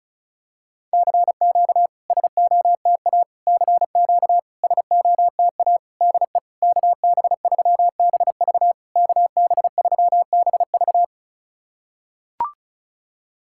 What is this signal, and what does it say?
Morse code, heard as a keyed tone, about 36 wpm, 700 Hz. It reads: CQ SOTA CQ SOTA DE KB3BV KB3BV
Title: SOTA QSO Practice 35wpm